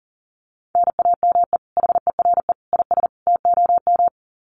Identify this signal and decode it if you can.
Morse code, heard as a keyed tone, about 40 wpm, 700 Hz. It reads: NAME HERE IS TOM
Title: Common QSO Phrases 40wpm